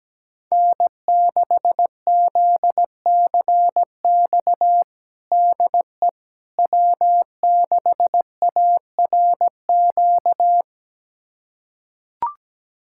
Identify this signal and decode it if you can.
Morse code, heard as a keyed tone, about 17 wpm, 700 Hz. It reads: N6ZCX DE W6ARQ
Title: USA Callsign DE Callsign - 5 Characters - 17wpm